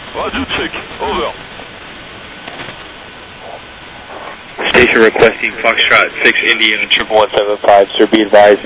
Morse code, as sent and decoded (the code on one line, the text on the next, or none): none